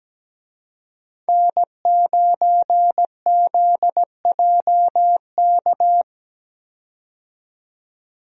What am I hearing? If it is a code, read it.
Morse code, heard as a keyed tone, about 17 wpm, 700 Hz. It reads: N9ZJK
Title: USA Call Signs - 5 Characters - 17wpm